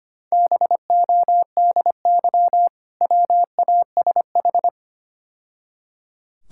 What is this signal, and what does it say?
Morse code, heard as a keyed tone, about 25 wpm, 690 Hz. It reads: BODY WASH
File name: Easy Words - Speed-Racing with Hints - 25wpm